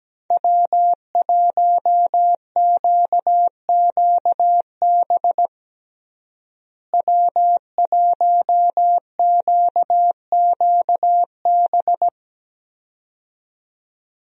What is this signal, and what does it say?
Morse code, heard as a keyed tone, about 17 wpm, 700 Hz. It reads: W1QQB W1QQB
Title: USA Call Signs - 5 Characters - Repeated 2x - 17wpm